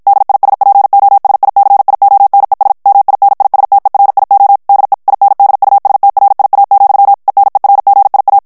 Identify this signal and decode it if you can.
Morse code, heard as a keyed tone, about 52 wpm, 800 Hz. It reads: DISPOSICIONES MINISTERIO DE INDUSTRIA, ENERGIA
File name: ESPAGNOL TEXTE EN CLAIR 2 VITESSE 50 MOTS MINUTES